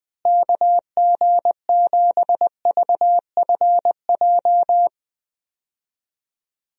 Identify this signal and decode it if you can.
Morse code, heard as a keyed tone, about 20 wpm, 700 Hz. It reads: KG7VFJ
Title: USA Call Signs - 6 Characters - Repeated 3x - 20wpm